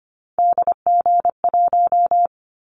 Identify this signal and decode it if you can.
Morse code, heard as a keyed tone, about 25 wpm, 700 Hz. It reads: DG1
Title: Callsign Prefixes for Top 25 DXCC - Repeated 3x - 25wpm